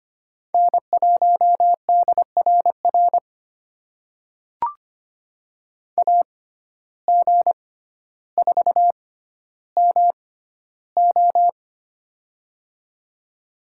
Morse code, heard as a keyed tone, about 25 wpm, spaced as for 13 wpm, 700 Hz. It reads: N1DRR AG4MO
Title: USA Call Signs - 5 Characters - 6x Character Spacing - 25wpm